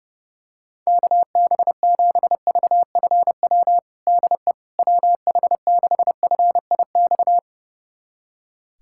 Morse code, heard as a keyed tone, about 30 wpm, 690 Hz. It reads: KB7VFW DE WH6FIX
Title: USA Callsign DE Callsign - 6 Characters - 30wpm